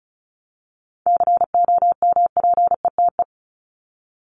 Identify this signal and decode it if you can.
Morse code, heard as a keyed tone, about 34 wpm, 700 Hz. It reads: COMPETE
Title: Top 1751-2000 Words 35wpm